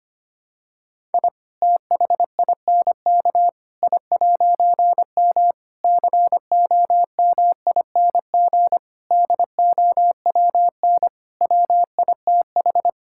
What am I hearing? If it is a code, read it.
Morse code, heard as a keyed tone, about 25 wpm, 700 Hz. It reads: I THINK I'M COMING DOWN WITH
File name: practice / 25wpm / All-Age Appropriate Jokes - 25wpm